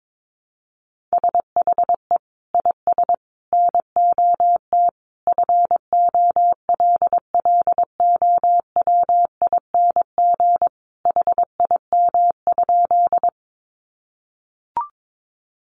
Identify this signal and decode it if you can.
Morse code, heard as a keyed tone, about 22 wpm, 700 Hz. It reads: SHE IS NOT FOLLOWING HIM?
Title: Sentences from Top 700 Words 22wpm